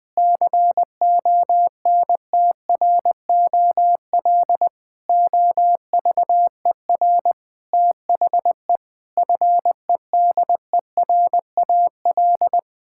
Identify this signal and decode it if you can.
Morse code, heard as a keyed tone, about 20 wpm, 700 Hz. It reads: CONTROL OVER THE FEDERAL
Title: Sentences from Top 1000 Words - Review Part 5 - 20wpm